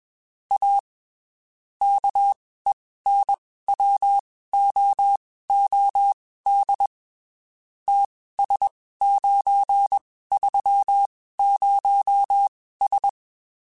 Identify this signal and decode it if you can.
Morse code, heard as a keyed tone, about 21 wpm, spaced as for 15 wpm, 800 Hz. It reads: A KENWOOD TS930S
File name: (spaced) / QSOs Spaced-mp3s / kc7sac @20 sp